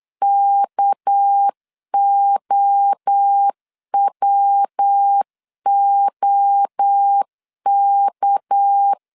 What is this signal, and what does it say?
Morse code, heard as a keyed tone, about 8 wpm, 800 Hz. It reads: KOWOK
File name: CW / DET / 1 6 1500C S B 8wpm 5142